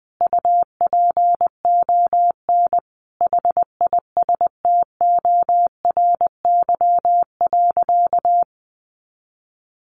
Morse code, heard as a keyed tone, about 20 wpm, 700 Hz. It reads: UPON HISTORY.
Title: Sentences from Top 1100 Words 20wpm